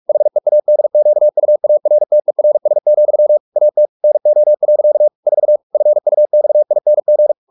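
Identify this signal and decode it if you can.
Morse code, heard as a keyed tone, about 44 wpm, 600 Hz. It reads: HEADQUARTERS, AT NO. 4 FUXING
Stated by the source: Morse practice at 45 wpm, the novel Never by Ken Follett in English, from CW med Gnister.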